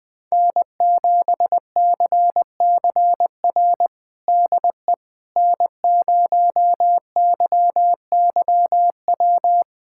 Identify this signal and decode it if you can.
Morse code, heard as a keyed tone, about 20 wpm, 700 Hz. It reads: N7CCR DE N0YYW
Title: USA Callsign DE Callsign - 5 Characters - 20wpm